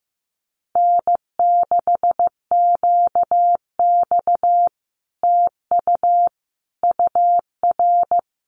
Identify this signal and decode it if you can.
Morse code, heard as a keyed tone, about 15 wpm, 700 Hz. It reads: N6QX TU UR
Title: Common QSO Phrases Encore 15wpm